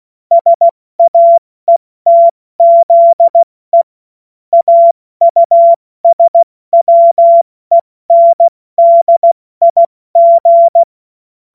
Morse code, heard as a keyed tone, about 16 wpm, 680 Hz. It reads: SAETZE AUSWENDIG
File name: Morsekurs / Witze / Witz 06. Juli 2021, 70 F80